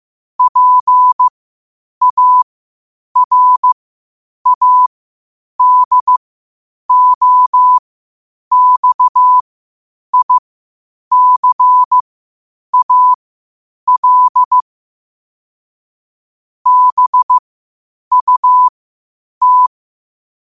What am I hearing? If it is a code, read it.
Morse code, heard as a keyed tone, about 15 wpm, spaced as for 8 wpm, 1000 Hz. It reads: PARADOXICAL BUT
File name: qotdfiles / 8wpm